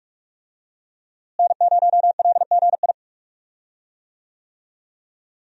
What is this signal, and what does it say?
Morse code, heard as a keyed tone, about 45 wpm, 700 Hz. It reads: N0LGI
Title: USA Call Signs - 5 Characters - Repeated 2x - 45wpm